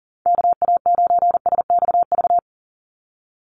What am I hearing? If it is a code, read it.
Morse code, heard as a keyed tone, about 40 wpm, 700 Hz. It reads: KA9SXV
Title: Call Signs - 40wpm